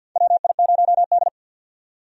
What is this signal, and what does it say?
Morse code, heard as a keyed tone, about 51 wpm, 700 Hz. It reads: WI0D
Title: USA Call Signs - 4 Characters - 50wpm